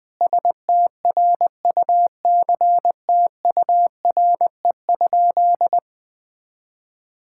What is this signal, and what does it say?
Morse code, heard as a keyed tone, about 21 wpm, 700 Hz. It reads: STRUCTURE?
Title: Sentences from Top 1000 Words - Review Part 6 - 20wpm